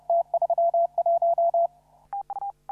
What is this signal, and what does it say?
Morse code, heard as a keyed tone, about 30 wpm, 680 Hz. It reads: T31
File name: CW / RHOB CW 2017 HA6P 0001